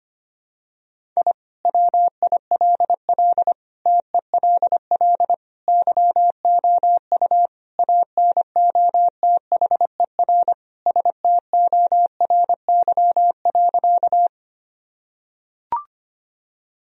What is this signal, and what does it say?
Morse code, heard as a keyed tone, about 25 wpm, 700 Hz. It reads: I WILL TELL YOU ANOTHER STORY.
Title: Sentences from Top 1000 Words - Review Part 1 - 25wpm